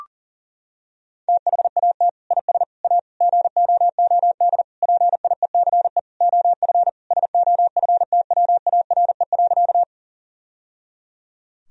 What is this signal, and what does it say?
Morse code, heard as a keyed tone, about 40 wpm, 700 Hz. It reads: THAT IS A GOOD PIECE OF SOFTWARE.
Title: Sentences from Top 1000 Words - Review Part 3 - 40wpm